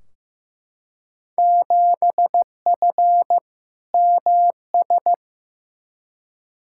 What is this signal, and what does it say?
Morse code, heard as a keyed tone, about 16 wpm, 700 Hz. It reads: 7F MS
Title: Field Day Exchanges Encore - 15wpm